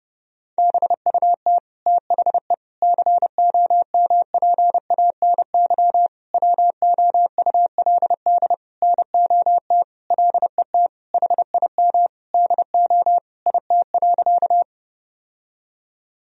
Morse code, heard as a keyed tone, about 30 wpm, 700 Hz. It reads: BUT THE COMPANY WOULD NOT LET HIM DO IT.
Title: Sentences from Top 1000 Words - Review Part 3 - 30wpm